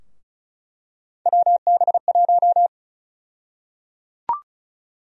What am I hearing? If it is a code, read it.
Morse code, heard as a keyed tone, about 34 wpm, 700 Hz. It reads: WB1
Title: USA Call Sign Prefixes - 35wpm